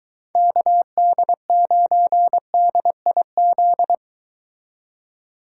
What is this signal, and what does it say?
Morse code, heard as a keyed tone, about 23 wpm, 700 Hz. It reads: KD9DIZ